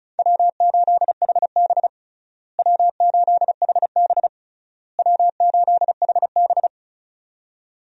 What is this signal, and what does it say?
Morse code, heard as a keyed tone, about 36 wpm, 700 Hz. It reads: W8HB W8HB W8HB
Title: USA Call Signs - 4 Characters - Repeated 3x - 35wpm